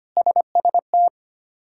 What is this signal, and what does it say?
Morse code, heard as a keyed tone, about 25 wpm, 700 Hz. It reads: SST
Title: K1USN SST QSO Practice - 25wpm